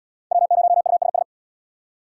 Morse code, heard as a keyed tone, about 77 wpm, 700 Hz. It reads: W0USS